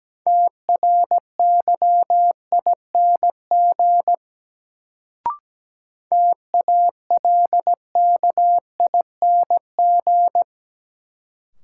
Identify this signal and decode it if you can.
Morse code, heard as a keyed tone, about 17 wpm, 700 Hz. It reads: TRYING TALKING
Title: Sets of 2 Words - New Word Drill - 17wpm